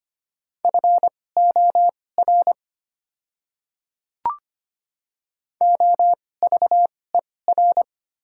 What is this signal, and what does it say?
Morse code, heard as a keyed tone, about 25 wpm, spaced as for 18 wpm, 690 Hz. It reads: FOR OVER
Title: 2x Character Spacing - Top 100 Words - 25wpm